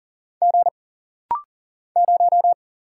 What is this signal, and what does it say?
Morse code, heard as a keyed tone, about 39 wpm, 700 Hz. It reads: G 0
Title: Single Letter-Number Alt - Rapid-Fire 40wpm